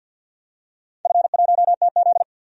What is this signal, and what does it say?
Morse code, heard as a keyed tone, about 53 wpm, 700 Hz. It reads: F1TZ
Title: DXCC-100 Call Signs - 4 Characters - 50wpm